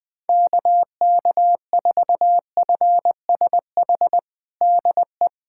Morse code, heard as a keyed tone, about 21 wpm, 700 Hz. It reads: KK4FSH DE
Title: USA Callsign DE Callsign - 6 Characters - 20wpm